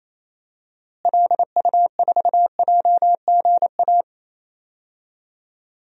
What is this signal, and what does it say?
Morse code, heard as a keyed tone, about 28 wpm, 700 Hz. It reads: LU4JGA
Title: DXCC-100 Call Signs - 6 Characters - Repeated 3x - 28wpm